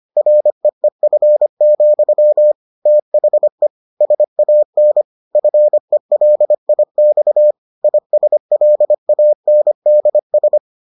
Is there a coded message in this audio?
Yes: Morse code from 0.2 to 10.6 s, heard as a keyed tone, about 25 wpm, 600 Hz. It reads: REEF, THE SAN FELIX ISLANDS